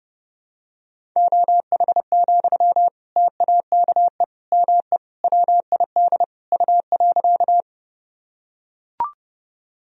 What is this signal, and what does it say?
Morse code, heard as a keyed tone, about 30 wpm, 700 Hz. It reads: OH, TAKE ME WID U.
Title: Sentences from Top 500 Words and Abbreviations 30wpm